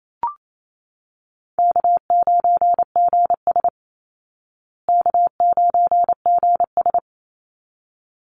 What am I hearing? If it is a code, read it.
Morse code, heard as a keyed tone, about 28 wpm, 700 Hz. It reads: K9GS K9GS
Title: Call Signs Repeated 3x - 28wpm